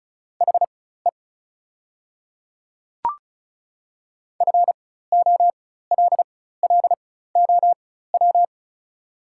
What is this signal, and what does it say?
Morse code, heard as a keyed tone, about 36 wpm, spaced as for 16 wpm, 700 Hz. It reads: HE FOLLOW
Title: 4x Character Spacing - Top 200 Words - 35wpm